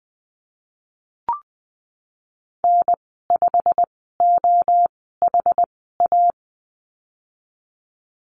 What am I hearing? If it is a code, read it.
Morse code, heard as a keyed tone, about 20 wpm, spaced as for 15 wpm, 700 Hz. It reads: N5OHA
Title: 167 Lesson 27g - Number 7 - Call Signs 20wpm